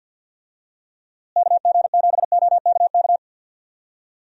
Morse code, heard as a keyed tone, about 51 wpm, 700 Hz. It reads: KG7OKK
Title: USA Call Signs - 6 Characters - Repeated 2x - 50wpm